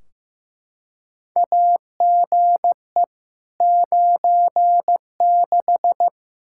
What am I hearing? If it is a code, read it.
Morse code, heard as a keyed tone, about 15 wpm, 700 Hz. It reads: AGE 96
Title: Practice QSO - Exchange 3 - More Info - 15wpm